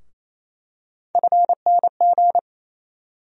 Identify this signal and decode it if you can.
Morse code, heard as a keyed tone, about 28 wpm, 700 Hz. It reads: FNG
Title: USA Call Sign Postfixes - 3 Letters 28wpm